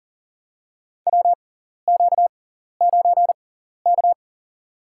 Morse code, heard as a keyed tone, about 40 wpm, spaced as for 14 wpm, 700 Hz. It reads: WQ9K
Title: USA Call Signs - 4 Characters - 6x Character Spacing - 40wpm